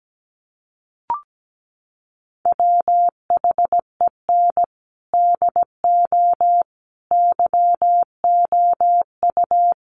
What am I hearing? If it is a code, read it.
Morse code, heard as a keyed tone, about 17 wpm, 700 Hz. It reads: WHEN DO YOU
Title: Sentences from Top 1100 Words 17wpm